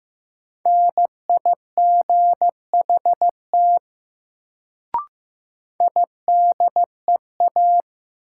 Morse code, heard as a keyed tone, about 15 wpm, 700 Hz. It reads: NIGHT IDEA
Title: Sentences from Top 300 Words - New Word Drill - 15wpm